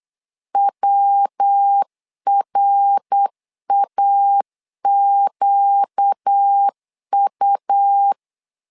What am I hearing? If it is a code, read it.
Morse code, heard as a keyed tone, about 8 wpm, 800 Hz. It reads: WRAQU
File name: LEZIONI / 4 Q 8wpm 2732